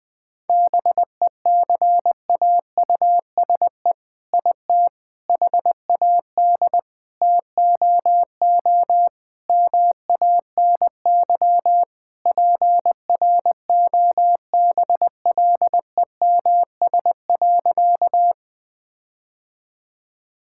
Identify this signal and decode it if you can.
Morse code, heard as a keyed tone, about 21 wpm, 700 Hz. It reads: BECAUSE IT HAD TOO MANY PROBLEMS.
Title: All-Age Appropriate Jokes - 20wpm